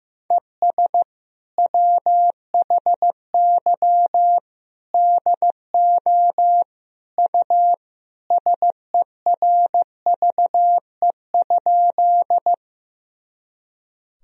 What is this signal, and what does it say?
Morse code, heard as a keyed tone, about 15 wpm, 700 Hz. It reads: ES WHY DO U SERVE?
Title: Sentences from Top 400 Words and Abbreviations 15wpm